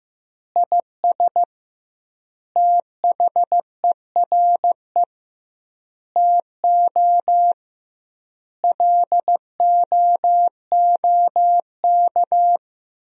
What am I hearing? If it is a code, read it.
Morse code, heard as a keyed tone, about 16 wpm, 700 Hz. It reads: IS THERE TO LOOK
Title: 2x Word-Spacing - Sentences from Top 100 Words - 15wpm